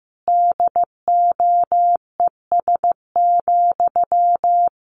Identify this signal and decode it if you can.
Morse code, heard as a keyed tone, about 15 wpm, 700 Hz. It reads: DOES,
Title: Sentences from Top 1500 Words 15wpm